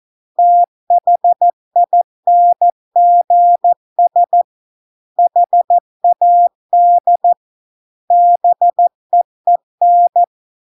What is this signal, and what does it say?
Morse code, heard as a keyed tone, about 14 wpm, 700 Hz. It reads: THINGS HAD BEEN